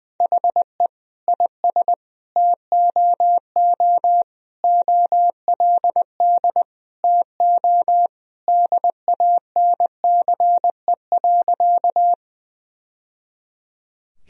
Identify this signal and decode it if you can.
Morse code, heard as a keyed tone, about 20 wpm, 700 Hz. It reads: HE IS TOO OLD TO DANCE.
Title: Sentences from Top 1000 Words - Review Part 4 - 20wpm